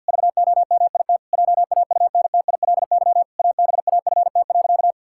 Morse code, heard as a keyed tone, about 52 wpm, 700 Hz. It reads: VOMIT JAUNTILY ABAFT.
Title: Pangram Challenge v2 - 50wpm